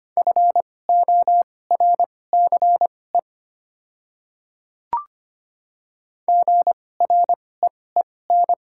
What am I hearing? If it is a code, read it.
Morse code, heard as a keyed tone, about 25 wpm, spaced as for 18 wpm, 700 Hz. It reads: FORCE GREEN
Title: 2x Character Spacing - Top 500 Words - 25wpm